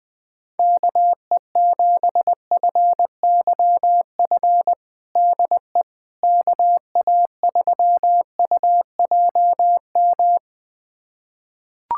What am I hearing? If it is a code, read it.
Morse code, heard as a keyed tone, about 20 wpm, 700 Hz. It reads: KE7FYF DE KA3UJM
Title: USA Callsign DE Callsign - 6 Characters - 20wpm